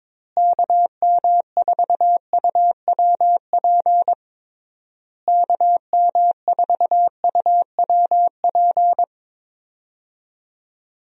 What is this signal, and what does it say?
Morse code, heard as a keyed tone, about 21 wpm, 700 Hz. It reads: KM4UWP KM4UWP
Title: USA Call Signs - 6 Characters - Repeated 3x - 22wpm